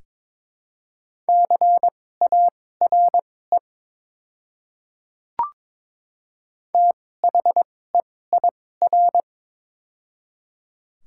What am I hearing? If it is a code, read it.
Morse code, heard as a keyed tone, about 22 wpm, spaced as for 16 wpm, 700 Hz. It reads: CARE THEIR
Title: 2x Character Spacing - Top 500 Words - 22wpm